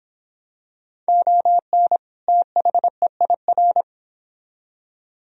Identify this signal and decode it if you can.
Morse code, heard as a keyed tone, about 25 wpm, 700 Hz. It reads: ON THEIR